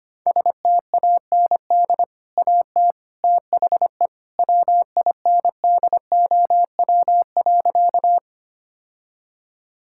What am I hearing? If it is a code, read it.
Morse code, heard as a keyed tone, about 25 wpm, 700 Hz. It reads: STAND AT THE WINDOW.
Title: Sentences from Top 1600 Words 25wpm